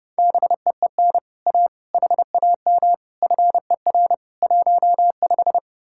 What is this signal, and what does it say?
Morse code, heard as a keyed tone, about 37 wpm, 700 Hz. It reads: BEEN A HAM FER 15
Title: Speed-Racing Common QSO Phrases - 30wpm